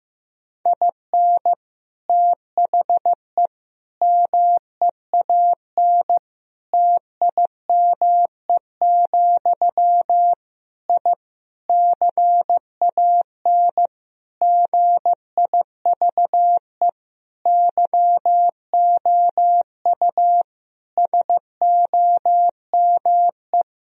Morse code, heard as a keyed tone, about 15 wpm, 700 Hz. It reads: IN THE MEAN TIME, I CAN GIVE YOU SOME
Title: Sentences from Top 1000 Words - Review Part 3 - 15wpm